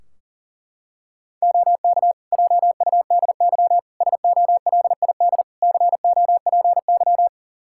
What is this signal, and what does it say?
Morse code, heard as a keyed tone, about 40 wpm, 700 Hz. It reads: OK JUDY SOLID COPY
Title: Practice QSO - Exchange 3 - More Info - 40wpm